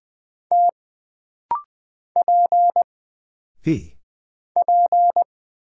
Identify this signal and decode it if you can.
Morse code, heard as a keyed tone, about 20 wpm, 700 Hz. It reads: T P P
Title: Single Character (Letter:Number:Punctuation:Prosign) - Rapid-Fire - 20wpm